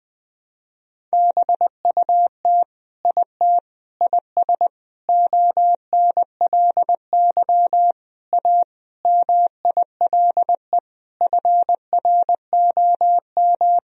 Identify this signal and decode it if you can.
Morse code, heard as a keyed tone, about 20 wpm, 700 Hz. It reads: BUT IT IS ONLY A MILE FROM
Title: Sentences from Top 1000 Words - Review Part 1 - 20wpm